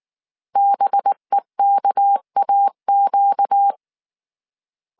Morse code, heard as a keyed tone, about 19 wpm, 800 Hz. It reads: BEKAQ
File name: CW / DET / 1 6 1500C S C 18wpm 2356